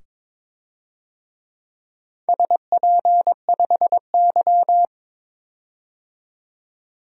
Morse code, heard as a keyed tone, about 22 wpm, 700 Hz. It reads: SP5Y
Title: DXCC-100 Call Signs - 4 Characters - 22wpm